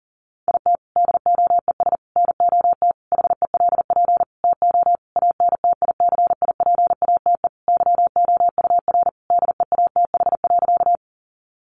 Morse code, heard as a keyed tone, about 40 wpm, 700 Hz. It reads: IT DOES NOT HELP TO ANTICIPATE YOUR DEATH.
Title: Sentences from Top 1500 Words 40wpm